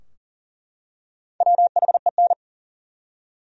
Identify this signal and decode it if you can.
Morse code, heard as a keyed tone, about 38 wpm, 700 Hz. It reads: WHEN
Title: Top 100 Words 40wpm